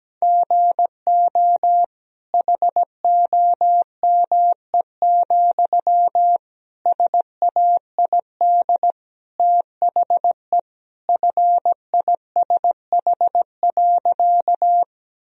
Morse code, heard as a keyed tone, about 17 wpm, 700 Hz. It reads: GO HOME, SAID THE FISH.
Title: Sentences from Top 1000 Words - Review Part 1 - 17wpm